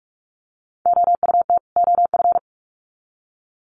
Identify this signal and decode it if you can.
Morse code, heard as a keyed tone, about 44 wpm, 700 Hz. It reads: OUT OF
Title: Sets of 2 Words 45wpm